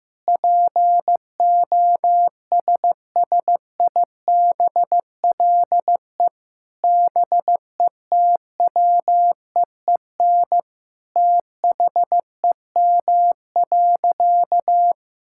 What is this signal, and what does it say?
Morse code, heard as a keyed tone, about 15 wpm, 700 Hz. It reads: POSSIBLE BETWEEN THEM.
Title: Sentences from Top 1000 Words - Review Part 5 - 15wpm